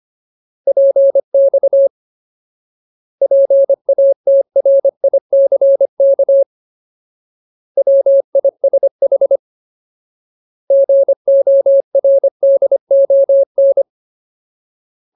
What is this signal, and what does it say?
Morse code, heard as a keyed tone, about 25 wpm, 550 Hz. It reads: PX PATRICK WISH GORDON
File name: Ultimate Word List / 25WPM Ultimate